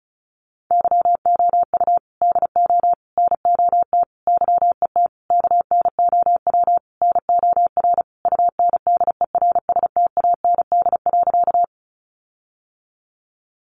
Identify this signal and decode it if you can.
Morse code, heard as a keyed tone, about 34 wpm, 700 Hz. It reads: YOU DO NOT YET KNOW NOR UNDERSTAND.
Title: Sentences from Top 1500 Words 35wpm